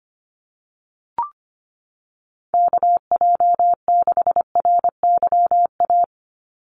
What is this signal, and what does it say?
Morse code, heard as a keyed tone, about 25 wpm, 700 Hz. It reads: KJ6RYA
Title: USA Call Signs 25wpm